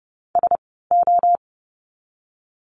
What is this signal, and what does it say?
Morse code, heard as a keyed tone, about 30 wpm, spaced as for 17 wpm, 700 Hz. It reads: SO
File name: Lesson 16f - Letter c - Words 30wpm